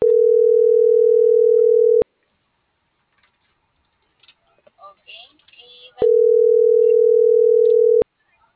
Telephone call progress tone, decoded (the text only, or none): ringback tone